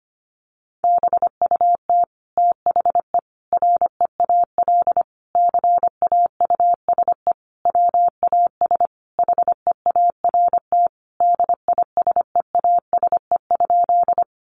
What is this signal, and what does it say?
Morse code, heard as a keyed tone, about 25 wpm, 700 Hz. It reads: BUT THE REAL CAUSE WAS HEART DISEASE?
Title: Sentences from Top 1200 Words 25wpm